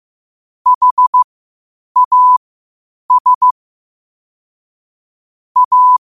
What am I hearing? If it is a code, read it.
Morse code, heard as a keyed tone, about 15 wpm, spaced as for 8 wpm, 1000 Hz. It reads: HAS A
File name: 8wpm